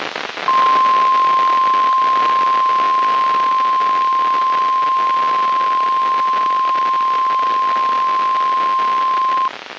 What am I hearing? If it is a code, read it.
Emergency Alert System (SAME) tones (the 1050 Hz attention tone), with noise. It is heard from 0.4 to 9.5 s.
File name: EAS WXM95 Towanda